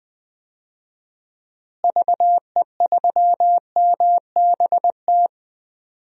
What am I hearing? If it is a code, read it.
Morse code, heard as a keyed tone, about 21 wpm, 700 Hz. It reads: VE3MBT
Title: DXCC-100 Call Signs - 6 Characters - Repeated 3x - 20wpm